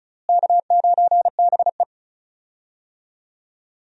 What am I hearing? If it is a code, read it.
Morse code, heard as a keyed tone, about 36 wpm, 700 Hz. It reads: K9BE
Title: USA Call Signs - 4 Characters - Repeated 3x - 35wpm